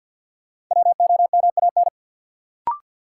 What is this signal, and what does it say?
Morse code, heard as a keyed tone, about 49 wpm, 700 Hz. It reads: WOMAN
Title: Sentences from Top 1000 Words - Review Part 2 - New Word Drill - 50wpm